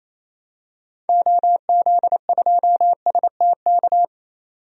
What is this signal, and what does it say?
Morse code, heard as a keyed tone, about 28 wpm, 700 Hz. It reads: OZ2STK
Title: DXCC-100 Call Signs - 6 Characters - 28wpm